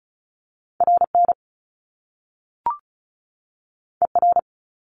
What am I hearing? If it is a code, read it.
Morse code, heard as a keyed tone, about 36 wpm, 700 Hz. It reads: RN ER
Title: Top Two Letter Combinations in Words 35wpm